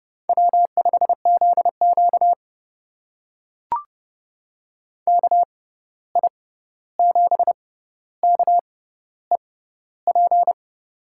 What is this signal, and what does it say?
Morse code, heard as a keyed tone, about 31 wpm, spaced as for 14 wpm, 700 Hz. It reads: W5ZQ KI7KEP
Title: USA Call Signs - 6x Character Spacing - 30wpm